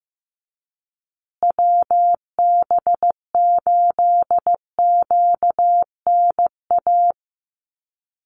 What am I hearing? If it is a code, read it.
Morse code, heard as a keyed tone, about 15 wpm, 700 Hz. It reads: WB8QNA
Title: USA Call Signs Repeated 3x 15wpm